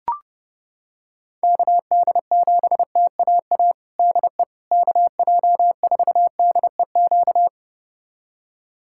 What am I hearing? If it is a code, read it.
Morse code, heard as a keyed tone, about 30 wpm, 700 Hz. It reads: KD7TAA DE KJ4DEQ
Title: USA Callsign DE Callsign 30wpm